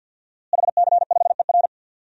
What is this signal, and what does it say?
Morse code, heard as a keyed tone, about 53 wpm, 700 Hz. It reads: HC5ER
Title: DXCC-100 Call Signs - 5 Characters - Repeated 2x - 50wpm